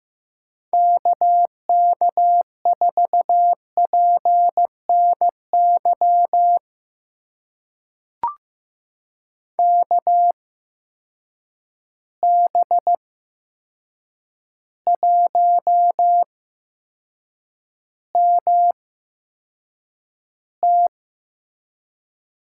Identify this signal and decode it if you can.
Morse code, heard as a keyed tone, about 16 wpm, spaced as for 7 wpm, 700 Hz. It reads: KK4PNY KB1MT
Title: USA Call Signs - 6 Characters - 8x Character Spacing - 15wpm